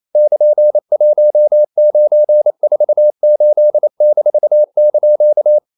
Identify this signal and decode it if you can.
Morse code, heard as a keyed tone, about 28 wpm, 600 Hz. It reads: <KN>1948-)
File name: CW / Practice / EnglishQuotations0128